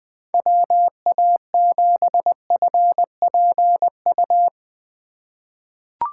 Morse code, heard as a keyed tone, about 20 wpm, 700 Hz. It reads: WA7FPU
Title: USA Call Signs - 6 Characters - 4x Character Spacing - 20wpm